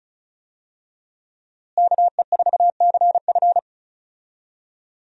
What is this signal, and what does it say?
Morse code, heard as a keyed tone, about 36 wpm, 700 Hz. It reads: KE4CF
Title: USA Call Signs - 5 Characters - Repeated 2x - 35wpm